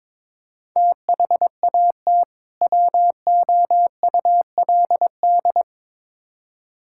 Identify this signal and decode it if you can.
Morse code, heard as a keyed tone, about 22 wpm, 700 Hz. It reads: THAT WOULD
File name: Build A Phrase - 2-Word Phrases - 22wpm